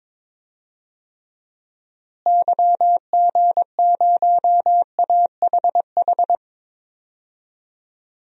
Morse code, heard as a keyed tone, about 22 wpm, 700 Hz. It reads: YG0AHH
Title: DXCC-100 Call Signs - 6 Characters - Repeated 3x - 22wpm